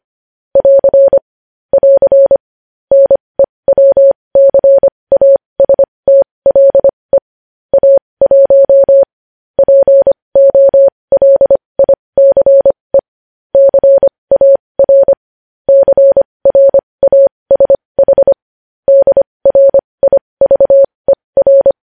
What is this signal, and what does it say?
Morse code, heard as a keyed tone, about 25 wpm, spaced as for 20 wpm, 550 Hz. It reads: <AR> <AR> NEWCASTLE A1 POLICE CAR CRASH DRIVER
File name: podcast / News headlines-1x-25x20